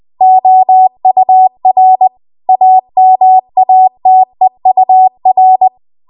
Morse code, heard as a keyed tone, about 20 wpm, 750 Hz. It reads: OUR AMATEUR
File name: Morse / 20WPM